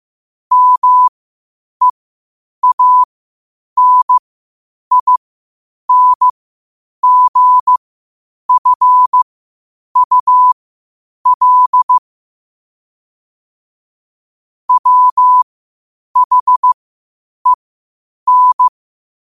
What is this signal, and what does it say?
Morse code, heard as a keyed tone, about 15 wpm, spaced as for 8 wpm, 1000 Hz. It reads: MEANINGFUL WHEN
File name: qotdfiles / 8wpm